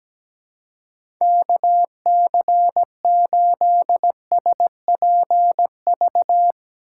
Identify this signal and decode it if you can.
Morse code, heard as a keyed tone, about 17 wpm, 700 Hz. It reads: KC8SPV
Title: USA Call Signs - 6 Characters - 4x Character Spacing - 17wpm